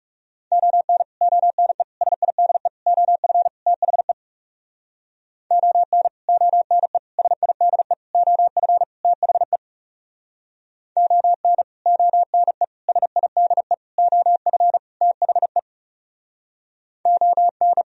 Morse code, heard as a keyed tone, about 39 wpm, 700 Hz. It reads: ON ONE SIDE OF THE ON ONE SIDE OF THE ON ONE SIDE OF THE ON